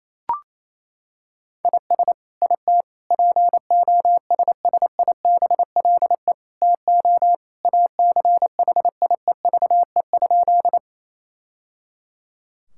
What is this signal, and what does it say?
Morse code, heard as a keyed tone, about 28 wpm, 700 Hz. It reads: IS IT POSSIBLE TO ACHIEVE?
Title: Sentences from Top 1000 Words - Review Part 6 - 28wpm